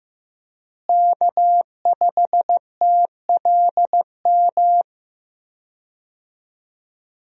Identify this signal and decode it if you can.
Morse code, heard as a keyed tone, about 15 wpm, 700 Hz. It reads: K5TLM
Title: USA Call Signs - 5 Characters - Repeated 3x - 15wpm